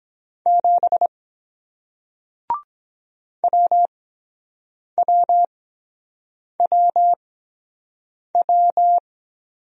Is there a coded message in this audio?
Yes: Morse code from 0.5 to 9.0 s, heard as a keyed tone, about 23 wpm, spaced as for 7 wpm, 700 Hz. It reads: 7 WWWW